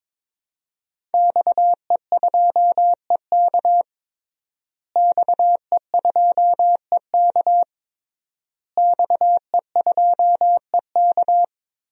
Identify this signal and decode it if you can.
Morse code, heard as a keyed tone, about 22 wpm, 700 Hz. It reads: XE2EK XE2EK XE2EK
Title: DXCC-100 Call Signs - 5 Characters - Repeated 3x - 22wpm